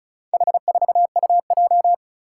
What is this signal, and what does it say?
Morse code, heard as a keyed tone, about 36 wpm, 700 Hz. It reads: H4UJ
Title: Four Challenging Letter-Number Combinations 35wpm